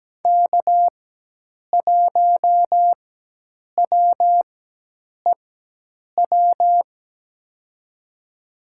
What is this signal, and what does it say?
Morse code, heard as a keyed tone, about 17 wpm, spaced as for 8 wpm, 700 Hz. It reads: K1WEW
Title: USA Call Signs - 4x Character Spacing - 17wpm